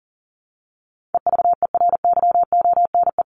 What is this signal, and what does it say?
Morse code, heard as a keyed tone, about 38 wpm, 700 Hz. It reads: EVERYONE
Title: Top 501-750 Words 40wpm